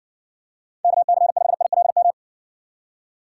Morse code, heard as a keyed tone, about 60 wpm, 700 Hz. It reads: KC5ILM